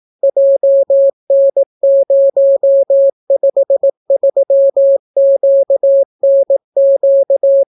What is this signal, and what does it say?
Morse code, heard as a keyed tone, about 18 wpm, 550 Hz. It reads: JN053QNQ